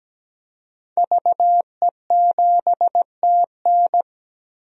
Morse code, heard as a keyed tone, about 17 wpm, 700 Hz. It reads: VE7TN
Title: DXCC-100 Call Signs - 5 Characters - Repeated 3x - 17wpm